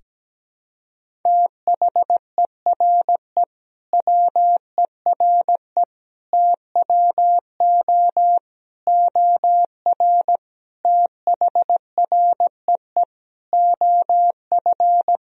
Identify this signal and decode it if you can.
Morse code, heard as a keyed tone, about 17 wpm, 700 Hz. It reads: THERE WERE TWO OR THREE OF
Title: Sentences from Top 1000 Words - Review Part 1 - 17wpm